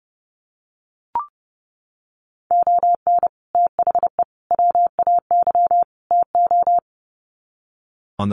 Morse code, heard as a keyed tone, about 30 wpm, 700 Hz. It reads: ON THE WAY TO
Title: Sets of 4 Words 30wpm